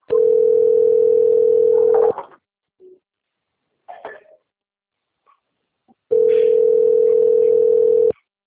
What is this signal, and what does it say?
Ringback tone (North America), 440 + 480 Hz.
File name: rg-1600-unknown- - - .21610